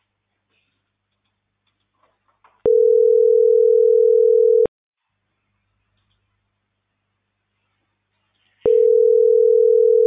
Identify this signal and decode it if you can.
Ringback tone (North America), 440 + 480 Hz.